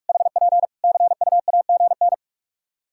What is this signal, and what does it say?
Morse code, heard as a keyed tone, about 48 wpm, 700 Hz. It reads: HP CUAGN
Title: Practice QSO - Exchange 4 - Ending - 45wpm